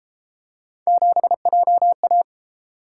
Morse code, heard as a keyed tone, about 29 wpm, 700 Hz. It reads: 7JA